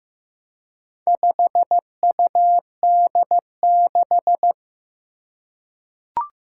Morse code, heard as a keyed tone, about 15 wpm, 700 Hz. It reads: 5UD6
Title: Four Challenging Letter-Number Combinations 15wpm